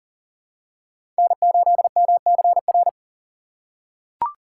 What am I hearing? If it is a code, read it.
Morse code, heard as a keyed tone, about 40 wpm, 700 Hz. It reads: N8MCR
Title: USA Call Signs - 5 Characters - 4x Character Spacing - 40wpm